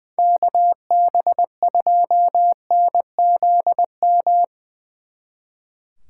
Morse code, heard as a keyed tone, about 22 wpm, 700 Hz. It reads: KB2NZM
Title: Speed Racing - USA 2x3 Call Signs 20wpm